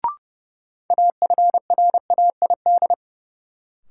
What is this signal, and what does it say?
Morse code, heard as a keyed tone, about 30 wpm, 700 Hz. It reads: AFRAID
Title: Sentences from Top 1500 Words - New Word Drill Alt - 30wpm